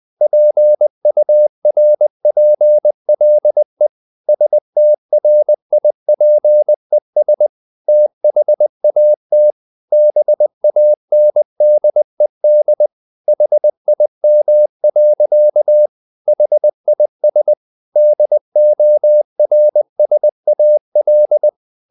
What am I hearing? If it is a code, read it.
Morse code, heard as a keyed tone, about 20 wpm, 600 Hz. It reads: PURPLE STRIPES THAT BANDED HIM. HIS DORSAL
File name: the old man and the sea part 2 20wpm